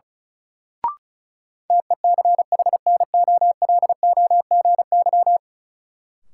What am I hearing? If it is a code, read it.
Morse code, heard as a keyed tone, about 34 wpm, 700 Hz. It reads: TECHNOLOGY
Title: Sentences from Top 1000 Words - Review Part 5 - New Word Drill - 35wpm